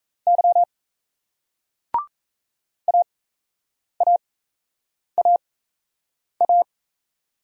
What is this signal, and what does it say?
Morse code, heard as a keyed tone, about 37 wpm, spaced as for 8 wpm, 700 Hz. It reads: Y AAAA